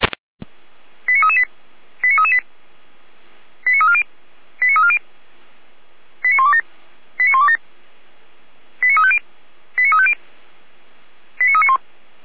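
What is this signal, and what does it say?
Five-tone selective call (ZVEI), heard as quick runs of five tones, tones of about 80 ms. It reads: 89338 89338 89340 89340 89127 89127 89350 89350 89391
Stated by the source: ZVEI five-tone call 89319 received by SDR, from the feuerwehr-orlamuende.org fire brigade audio archive.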